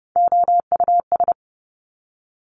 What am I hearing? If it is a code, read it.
Morse code, heard as a keyed tone, about 30 wpm, 700 Hz. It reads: OUS
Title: Top Three Letter Combinations in Words 30wpm